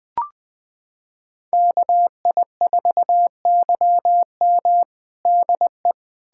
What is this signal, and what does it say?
Morse code, heard as a keyed tone, about 20 wpm, 700 Hz. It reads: KI4YM DE
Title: USA Callsign DE Callsign - 5 Characters - 20wpm